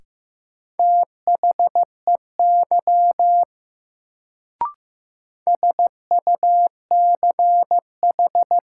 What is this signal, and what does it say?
Morse code, heard as a keyed tone, about 15 wpm, 700 Hz. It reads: THEY SUCH
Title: Sets of 2 Words - New Word Drill - 15wpm